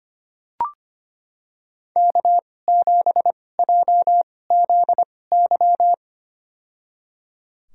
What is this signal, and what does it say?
Morse code, heard as a keyed tone, about 25 wpm, spaced as for 18 wpm, 700 Hz. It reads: K7JZY
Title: USA Call Signs - 2x Character Spacing - 25wpm